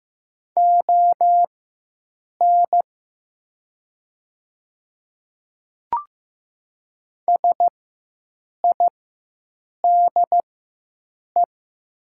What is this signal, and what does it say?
Morse code, heard as a keyed tone, about 15 wpm, spaced as for 7 wpm, 700 Hz. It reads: ON SIDE
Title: 4x Character Spacing - Top 100 Words - 15wpm